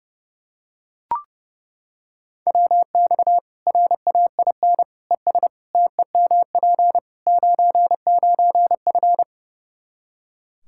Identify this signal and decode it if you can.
Morse code, heard as a keyed tone, about 30 wpm, 700 Hz. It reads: WX RAIN ES TEMP 99F
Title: Practice QSO - Exchange 2 - Info - 30wpm